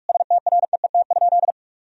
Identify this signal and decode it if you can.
Morse code, heard as a keyed tone, about 46 wpm, 700 Hz. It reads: STREET?
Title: Sentences from Top 1000 Words - Review Part 3 - 45wpm